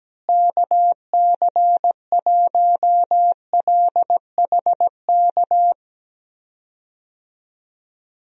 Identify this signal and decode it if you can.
Morse code, heard as a keyed tone, about 17 wpm, 700 Hz. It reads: KC1LHK
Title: USA Call Signs - 6 Characters - Repeated 3x - 17wpm